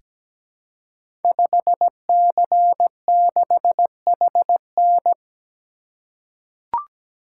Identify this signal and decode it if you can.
Morse code, heard as a keyed tone, about 17 wpm, 700 Hz. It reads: 5C6HN
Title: Code Groups - 5 Letter:Numbers - 17wpm